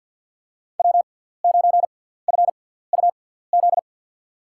Morse code, heard as a keyed tone, about 47 wpm, spaced as for 17 wpm, 700 Hz. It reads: W9FUZ
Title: USA Call Signs - 6x Character Spacing - 50wpm